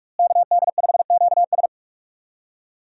Morse code, heard as a keyed tone, about 46 wpm, 700 Hz. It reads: KD5QS
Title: USA Call Signs - 5 Characters - Repeated 2x - 45wpm